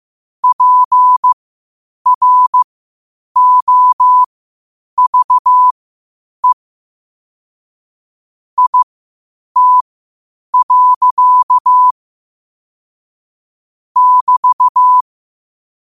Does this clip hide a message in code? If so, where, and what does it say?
Yes: Morse code from 0.4 to 15.0 s, heard as a keyed tone, about 15 wpm, spaced as for 8 wpm, 1000 Hz. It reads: PROVE IT. <BT>